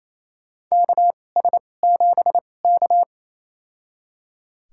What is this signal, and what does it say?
Morse code, heard as a keyed tone, about 28 wpm, spaced as for 20 wpm, 700 Hz. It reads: KS7K
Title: USA Call Signs - 4 Characters - 2x Character Spacing - 28wpm